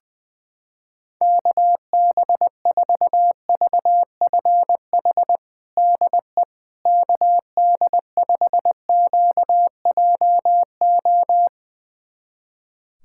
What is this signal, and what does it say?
Morse code, heard as a keyed tone, about 20 wpm, 700 Hz. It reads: KB4VFH DE KD5QJO
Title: USA Callsign DE Callsign 20wpm